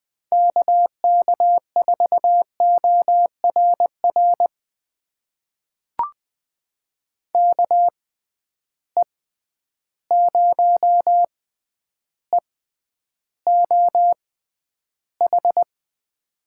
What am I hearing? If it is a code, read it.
Morse code, heard as a keyed tone, about 20 wpm, spaced as for 10 wpm, 700 Hz. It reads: KK4ORR KE0EOH
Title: USA Call Signs - 6 Characters - 6x Character Spacing - 20wpm